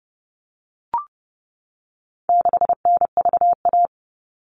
Morse code, heard as a keyed tone, about 30 wpm, 700 Hz. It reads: 6NVA
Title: All Cut Numbers - 4-Digit 30wpm